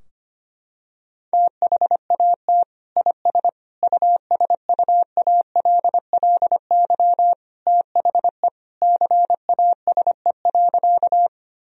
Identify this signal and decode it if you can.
Morse code, heard as a keyed tone, about 25 wpm, 700 Hz. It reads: THAT IS USUALLY THE CASE.
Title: Sentences from Top 1000 Words - Review Part 3 - 25wpm